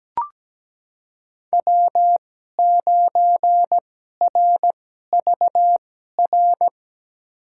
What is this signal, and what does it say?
Morse code, heard as a keyed tone, about 17 wpm, spaced as for 12 wpm, 700 Hz. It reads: W9RVR
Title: USA Call Signs - 2x Character Spacing - 17wpm